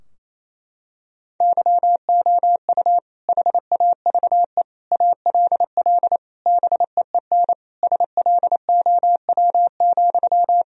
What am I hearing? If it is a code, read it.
Morse code, heard as a keyed tone, about 28 wpm, 700 Hz. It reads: YOU HAVE ALL BEEN SLOW,
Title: Sentences from Top 1000 Words - Review Part 2 - 28wpm